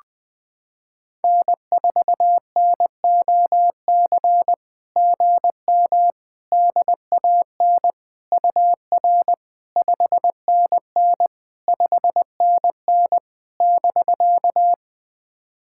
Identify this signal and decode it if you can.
Morse code, heard as a keyed tone, about 20 wpm, 700 Hz. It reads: N4NOC GM DAN UR 5NN 5NN <BK>
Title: SOTA QSO Practice 20wpm